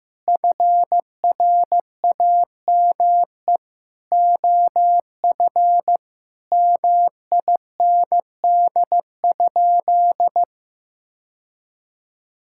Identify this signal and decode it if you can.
Morse code, heard as a keyed tone, about 15 wpm, 700 Hz. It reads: FRAME OF MIND?
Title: Sentences from Top 1000 Words - Review Part 6 - 15wpm